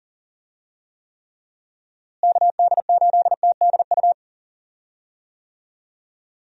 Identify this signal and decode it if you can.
Morse code, heard as a keyed tone, about 40 wpm, 700 Hz. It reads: KD8TDU
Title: USA Call Signs - 6 Characters - Repeated 3x - 40wpm